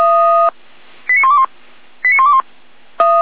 Five-tone selective call (ZVEI), heard as quick runs of five tones, tones of about 70 ms. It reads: 89121 89121
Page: /455/SDR/ZVEI/89119/